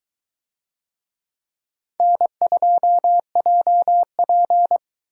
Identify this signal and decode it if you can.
Morse code, heard as a keyed tone, about 22 wpm, 700 Hz. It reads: N2JP